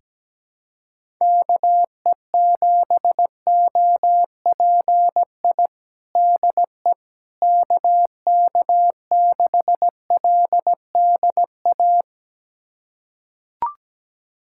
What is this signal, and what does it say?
Morse code, heard as a keyed tone, about 17 wpm, 700 Hz. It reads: KE7OPI DE KK6LDA
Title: USA Callsign DE Callsign - 6 Characters - 17wpm